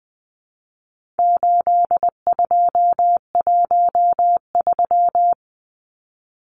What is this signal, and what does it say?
Morse code, heard as a keyed tone, about 20 wpm, 700 Hz. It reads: 8213
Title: Numbers - 4-Digit 20wpm